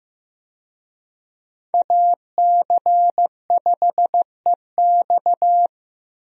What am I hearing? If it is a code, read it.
Morse code, heard as a keyed tone, about 15 wpm, 700 Hz. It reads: AC5EX
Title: USA Call Signs - 5 Characters - Repeated 2x - 15wpm